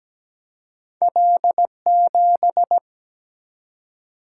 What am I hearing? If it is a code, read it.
Morse code, heard as a keyed tone, about 17 wpm, 700 Hz. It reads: L7
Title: Single-Letter Single-Number 17wpm